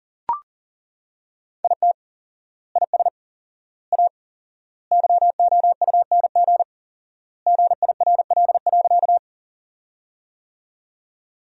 Morse code, heard as a keyed tone, about 40 wpm, 700 Hz. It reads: IT IS A YOUNG GIRL.
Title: 4x Word-Spacing - Sentences from Top 400 Words - 40wpm